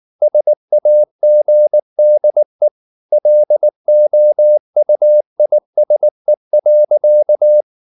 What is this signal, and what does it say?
Morse code, heard as a keyed tone, about 19 wpm, 600 Hz. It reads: SAGDE LOUISE.
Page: Ferie 19 wpm